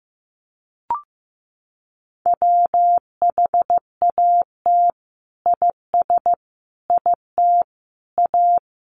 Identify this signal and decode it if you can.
Morse code, heard as a keyed tone, about 15 wpm, 700 Hz. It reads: WHAT IS IT A
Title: Sentences from Top 1300 Words 15wpm